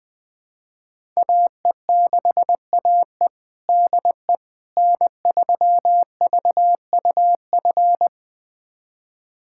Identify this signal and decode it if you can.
Morse code, heard as a keyed tone, about 21 wpm, 700 Hz. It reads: AE6AE DE N3VUF
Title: USA Callsign DE Callsign - 5 Characters - 20wpm